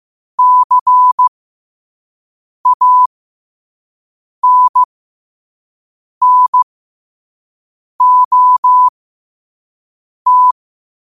Morse code, heard as a keyed tone, about 15 wpm, spaced as for 5 wpm, 1000 Hz. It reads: CANNOT
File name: qotdfiles / 5wpm